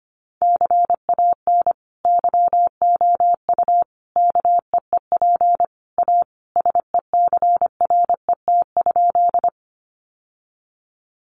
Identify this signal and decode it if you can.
Morse code, heard as a keyed tone, about 25 wpm, 700 Hz. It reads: CAN YOU KEEP A SECRET?
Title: Sentences from Top 1200 Words 25wpm